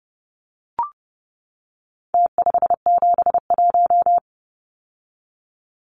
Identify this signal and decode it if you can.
Morse code, heard as a keyed tone, about 30 wpm, 700 Hz. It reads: T571
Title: Numbers - Contest Style - 4-Digit 30wpm